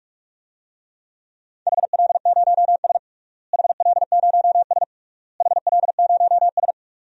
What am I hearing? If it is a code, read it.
Morse code, heard as a keyed tone, about 46 wpm, 700 Hz. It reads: HL0S HL0S HL0S
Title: DXCC-100 Call Signs - 4 Characters - Repeated 3x - 45wpm